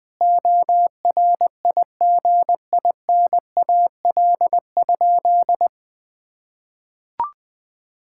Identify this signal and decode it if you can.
Morse code, heard as a keyed tone, about 21 wpm, 700 Hz. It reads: ORIGINAL?
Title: Sentences from Top 1000 Words - Review Part 6 - 20wpm